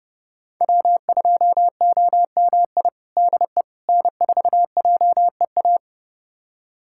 Morse code, heard as a keyed tone, about 32 wpm, 700 Hz. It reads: W2OMI DE N4JEA
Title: USA Callsign DE Callsign - 5 Characters - 30wpm